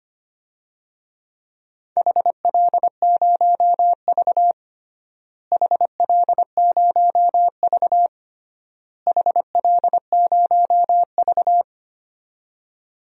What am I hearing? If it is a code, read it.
Morse code, heard as a keyed tone, about 25 wpm, 700 Hz. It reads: HL0V HL0V HL0V
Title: DXCC-100 Call Signs - 4 Characters - Repeated 3x - 25wpm